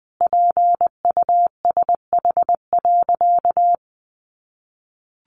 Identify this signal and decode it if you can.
Morse code, heard as a keyed tone, about 20 wpm, 700 Hz. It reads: PUSH.
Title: Sentences from Top 1300 Words 20wpm